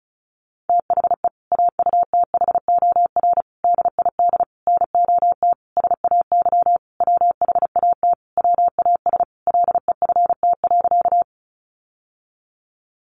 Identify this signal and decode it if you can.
Morse code, heard as a keyed tone, about 34 wpm, 700 Hz. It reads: THE AUTHOR DID NOT SAY WHAT WAS LEFT.
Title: Sentences from Top 1200 Words 35wpm